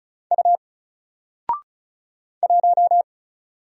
Morse code, heard as a keyed tone, about 36 wpm, 700 Hz. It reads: U 1
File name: Single-Letter-Number-35wpm